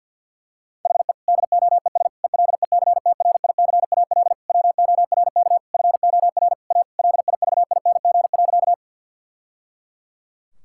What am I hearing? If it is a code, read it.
Morse code, heard as a keyed tone, about 50 wpm, 700 Hz. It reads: HE DOES ELECTRICAL WORK FOR A LIVING.
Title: Sentences from Top 1000 Words - Review Part 6 - 50wpm